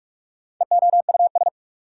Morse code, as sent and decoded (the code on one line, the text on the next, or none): . --- ..- ...
EOUS